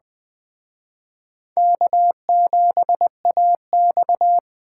Morse code, heard as a keyed tone, about 21 wpm, 700 Hz. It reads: K7AX